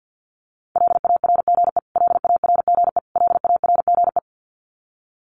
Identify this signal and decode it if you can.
Morse code, heard as a keyed tone, about 50 wpm, 700 Hz. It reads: LARGE LARGE LARGE
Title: Top 200 Words Repeated 50wpm